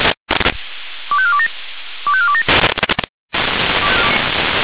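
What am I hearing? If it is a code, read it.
Five-tone selective call (ZVEI), heard as quick runs of five tones, tones of about 70 ms. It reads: 25427 25427
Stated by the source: ZVEI five-tone call received by SDR.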